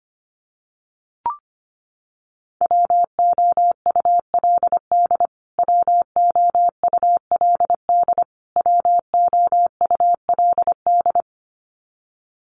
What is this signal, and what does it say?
Morse code, heard as a keyed tone, about 25 wpm, 700 Hz. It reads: WOULD WOULD WOULD
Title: Top 500 Words Repeated 25wpm